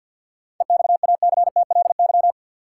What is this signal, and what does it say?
Morse code, heard as a keyed tone, about 50 wpm, 700 Hz. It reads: EXACTLY
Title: Seven Letter Words - 50wpm